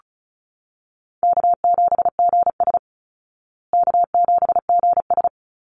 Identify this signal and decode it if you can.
Morse code, heard as a keyed tone, about 34 wpm, 700 Hz. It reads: K7GS K7GS
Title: Easy Call Signs Repeated 3x - 35wpm